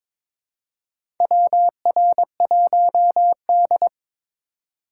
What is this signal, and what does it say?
Morse code, heard as a keyed tone, about 21 wpm, 700 Hz. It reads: WR1D
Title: USA Call Signs - 2x Character Spacing - 22wpm